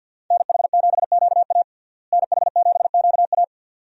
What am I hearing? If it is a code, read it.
Morse code, heard as a keyed tone, about 50 wpm, 700 Hz. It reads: NH7QA NH7QA
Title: USA Call Signs - 5 Characters - Repeated 2x - 50wpm